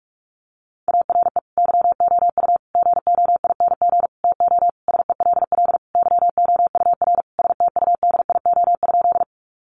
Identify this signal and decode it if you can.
Morse code, heard as a keyed tone, about 44 wpm, 700 Hz. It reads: ARE YOU GOING TO SELL YOUR STUDIO?
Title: Sentences from Top 1200 Words 45wpm